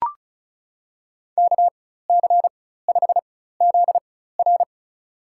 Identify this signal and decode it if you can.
Morse code, heard as a keyed tone, about 34 wpm, spaced as for 16 wpm, 700 Hz. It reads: KC5ZR
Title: USA Call Signs - 5 Characters - 4x Character Spacing - 35wpm